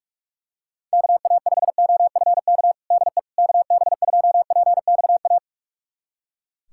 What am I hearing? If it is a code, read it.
Morse code, heard as a keyed tone, about 41 wpm, 700 Hz. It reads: KA5OFK DE KB2PXA
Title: USA Callsign DE Callsign - 6 Characters - 45wpm